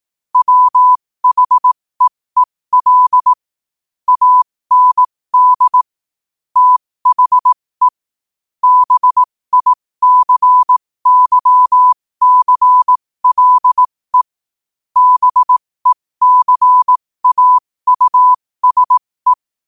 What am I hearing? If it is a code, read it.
Morse code, heard as a keyed tone, about 18 wpm, spaced as for 15 wpm, 1000 Hz. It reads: WHEEL AND THE BICYCLE BECAUSE
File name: qotdfiles / 15wpm